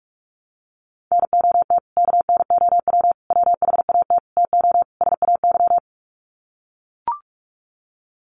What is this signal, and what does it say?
Morse code, heard as a keyed tone, about 44 wpm, 700 Hz. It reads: NOT KNOW WHAT TO SAY
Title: Sets of 5 Words 45wpm